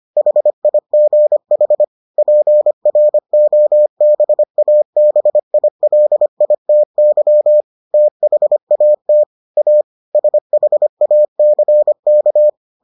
Morse code, heard as a keyed tone, about 25 wpm, 600 Hz. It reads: HIGH PROBABILITY THAT A SHACK